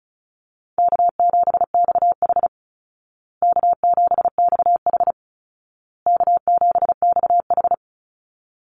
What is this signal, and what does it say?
Morse code, heard as a keyed tone, about 34 wpm, 700 Hz. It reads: K7XH K7XH K7XH
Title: Call Signs Repeated 2x - 35wpm